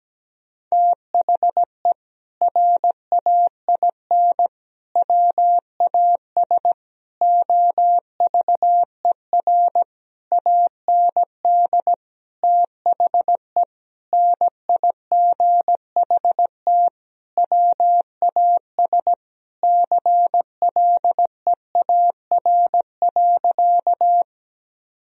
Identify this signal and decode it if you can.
Morse code, heard as a keyed tone, about 17 wpm, 700 Hz. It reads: THE RAIN WAS OVER AND THE NIGHT WAS CLEAR.
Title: Sentences from Top 1000 Words - Review Part 1 - 17wpm